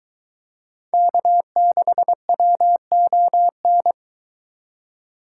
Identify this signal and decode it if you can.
Morse code, heard as a keyed tone, about 22 wpm, 700 Hz. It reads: K6WON